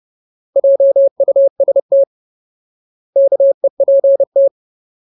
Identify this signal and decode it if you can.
Morse code, heard as a keyed tone, about 30 wpm, 550 Hz. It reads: JUST KEPT
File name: Ultimate Word List / 30WPM Ultimate